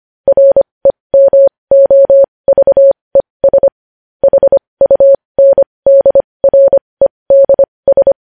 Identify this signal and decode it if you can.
Morse code, heard as a keyed tone, about 25 wpm, spaced as for 20 wpm, 550 Hz. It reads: REMOVES HUNDREDS
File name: podcast / News headlines-1x-25x20